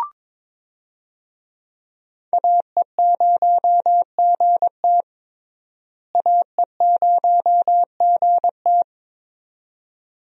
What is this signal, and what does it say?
Morse code, heard as a keyed tone, about 22 wpm, 700 Hz. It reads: AE0GT AE0GT
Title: USA Call Signs - 5 Characters - Repeated 2x - 22wpm